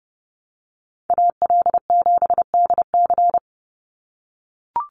Morse code, heard as a keyed tone, about 30 wpm, 700 Hz. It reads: AL7DC
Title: USA Call Signs 30wpm